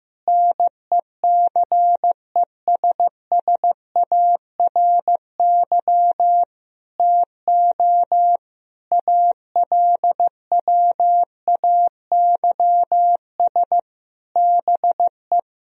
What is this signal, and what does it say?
Morse code, heard as a keyed tone, about 15 wpm, 700 Hz. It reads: NECESSARY TO ALWAYS BE
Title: Sentences from Top 1000 Words - Review Part 4 - 15wpm